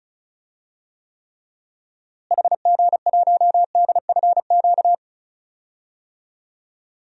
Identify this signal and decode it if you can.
Morse code, heard as a keyed tone, about 34 wpm, 700 Hz. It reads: HG1DFQ
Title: DXCC-100 Call Signs - 6 Characters - Repeated 3x - 35wpm